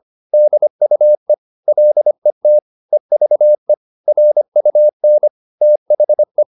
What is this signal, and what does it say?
Morse code, heard as a keyed tone, about 25 wpm, 600 Hz. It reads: DUE LET EVE RUN THE